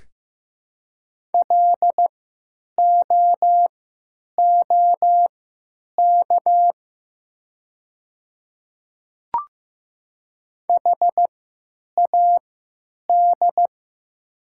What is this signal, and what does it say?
Morse code, heard as a keyed tone, about 16 wpm, spaced as for 9 wpm, 700 Hz. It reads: LOOK HAD
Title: 3x Character Spacing - Top 100 Words - 15wpm